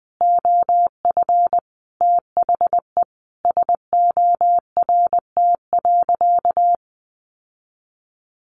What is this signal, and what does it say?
Morse code, heard as a keyed tone, about 20 wpm, 700 Hz. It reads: OF THE SORT.
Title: Sentences from Top 1300 Words 20wpm